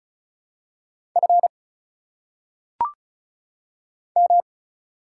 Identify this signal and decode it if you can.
Morse code, heard as a keyed tone, about 34 wpm, 700 Hz. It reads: F M
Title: Single Letters - Phonetic Pronunciation 35wpm